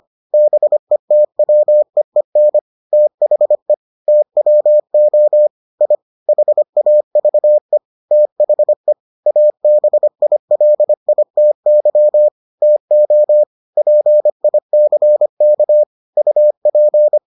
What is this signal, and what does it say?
Morse code, heard as a keyed tone, about 25 wpm, 600 Hz. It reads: BETWEEN THE TWO I HAVE THE ABILITY TO PICK UP